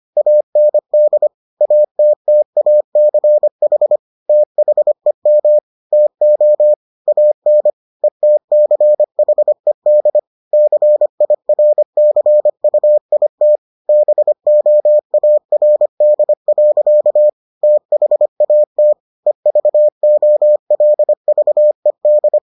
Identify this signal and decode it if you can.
Morse code, heard as a keyed tone, about 25 wpm, 600 Hz. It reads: AND ATTACH THEM TO AN ETCHED CIRCUIT BOARD. THAT EVOLVED